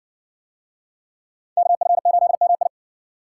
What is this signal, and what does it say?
Morse code, heard as a keyed tone, about 60 wpm, 700 Hz. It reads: KF8MI